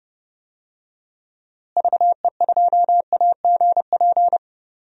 Morse code, heard as a keyed tone, about 30 wpm, 700 Hz. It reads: VE2AGP
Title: DXCC-100 Call Signs - 6 Characters - Repeated 2x - 30wpm